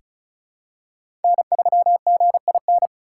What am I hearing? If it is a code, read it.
Morse code, heard as a keyed tone, about 36 wpm, 700 Hz. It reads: N3GIN
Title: USA Call Signs - 3x Character Spacing - 35wpm